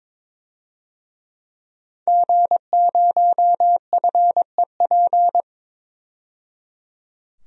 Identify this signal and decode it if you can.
Morse code, heard as a keyed tone, about 21 wpm, 700 Hz. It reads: G0FEP
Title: DXCC-100 Call Signs - 5 Characters - 22wpm